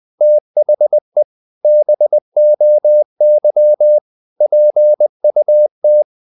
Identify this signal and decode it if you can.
Morse code, heard as a keyed tone, about 20 wpm, 600 Hz. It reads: THE BOY PUT
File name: the old man and the sea / the old man and the sea part 1 20wpm